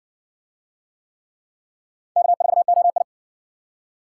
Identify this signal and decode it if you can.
Morse code, heard as a keyed tone, about 60 wpm, 700 Hz. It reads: K4OI